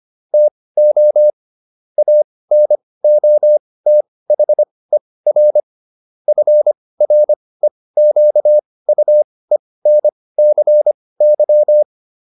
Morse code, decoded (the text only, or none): TO ANOTHER FREQUENCY